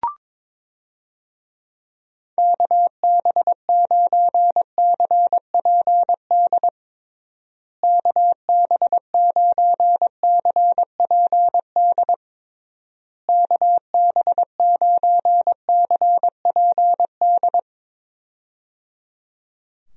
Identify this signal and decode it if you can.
Morse code, heard as a keyed tone, about 22 wpm, 700 Hz. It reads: KB9CPD KB9CPD KB9CPD
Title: USA Call Signs - 6 Characters - Repeated 3x - 22wpm